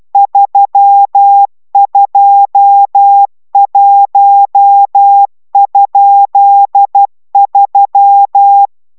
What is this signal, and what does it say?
Morse code, heard as a keyed tone, about 12 wpm, 800 Hz. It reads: 321?3